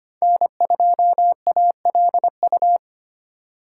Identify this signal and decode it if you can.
Morse code, heard as a keyed tone, about 25 wpm, 700 Hz. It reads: N2ALU
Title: USA Call Signs - 5 Characters - 4x Character Spacing - 25wpm